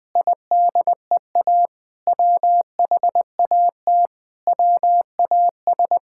Morse code, heard as a keyed tone, about 20 wpm, 700 Hz. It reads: IDEA WHAT WAS
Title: Sentences with 7-Word Phrases - Spoken Beginning - 20wpm